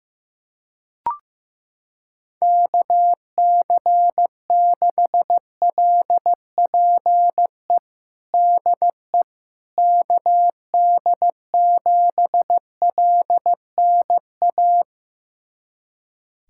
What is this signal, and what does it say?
Morse code, heard as a keyed tone, about 16 wpm, 700 Hz. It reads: KC6LPE DE KD7LNA
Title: USA Callsign DE Callsign - 6 Characters - 15wpm